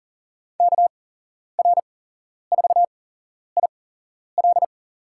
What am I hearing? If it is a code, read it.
Morse code, heard as a keyed tone, about 40 wpm, spaced as for 11 wpm, 700 Hz. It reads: KR4IL
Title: USA Call Signs - 8x Character Spacing - 40wpm